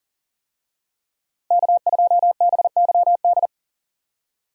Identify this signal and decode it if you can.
Morse code, heard as a keyed tone, about 40 wpm, 700 Hz. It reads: K2BYD
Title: USA Call Signs - 5 Characters - Repeated 2x - 40wpm